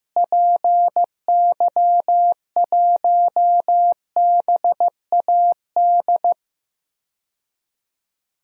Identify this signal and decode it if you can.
Morse code, heard as a keyed tone, about 15 wpm, 700 Hz. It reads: PY1BAD
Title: DXCC-100 Call Signs - 6 Characters - Repeated 3x - 15wpm